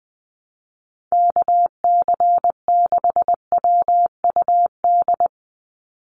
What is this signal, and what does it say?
Morse code, heard as a keyed tone, about 19 wpm, 700 Hz. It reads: KC6WUD
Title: USA Call Signs 20wpm